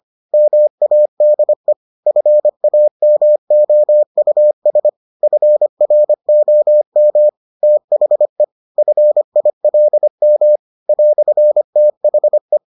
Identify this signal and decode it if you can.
Morse code, heard as a keyed tone, about 25 wpm, 600 Hz. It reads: MADE FAMOUS FROM THE FILM "THE